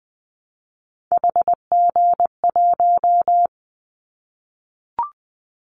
Morse code, heard as a keyed tone, about 20 wpm, 700 Hz. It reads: HG1
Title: Callsign Prefixes for Top 25 DXCC - Repeated 3x - 20wpm